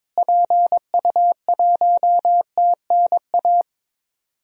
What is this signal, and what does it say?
Morse code, heard as a keyed tone, about 22 wpm, 700 Hz. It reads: PU1TNA
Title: DXCC-100 Call Signs - 6 Characters - Repeated 2x - 22wpm